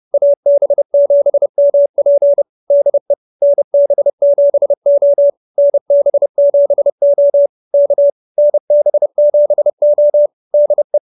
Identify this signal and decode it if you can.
Morse code, heard as a keyed tone, about 30 wpm, 590 Hz. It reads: AB7MP DE NB7O NB7O K NB7O DE